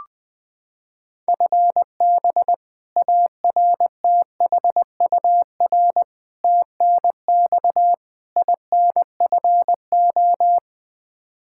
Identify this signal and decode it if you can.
Morse code, heard as a keyed tone, about 20 wpm, 700 Hz. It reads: FB ARTHUR TNX INFO
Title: Practice QSO - Exchange 3 - More Info - 20wpm